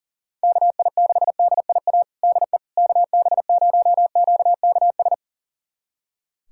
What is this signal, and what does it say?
Morse code, heard as a keyed tone, about 40 wpm, 700 Hz. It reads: KI6DIA DE KB0QKS
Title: USA Callsign DE Callsign 40wpm